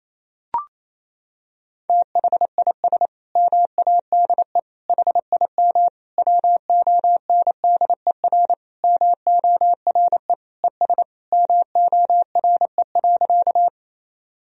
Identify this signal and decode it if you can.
Morse code, heard as a keyed tone, about 28 wpm, 700 Hz. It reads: THIS MADE HIM WONDER MORE ES MORE.
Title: Sentences from Top 500 Words and Abbreviations 28wpm